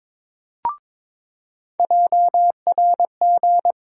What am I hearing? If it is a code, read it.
Morse code, heard as a keyed tone, about 22 wpm, 700 Hz. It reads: JRG
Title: USA Call Sign Postfixes - 3 Letters 22wpm